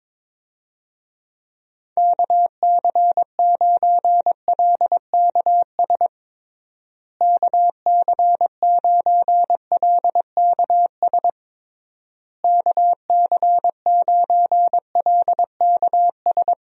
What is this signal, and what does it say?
Morse code, heard as a keyed tone, about 22 wpm, 700 Hz. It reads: KC9LKS KC9LKS KC9LKS
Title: USA Call Signs - 6 Characters - Repeated 3x - 22wpm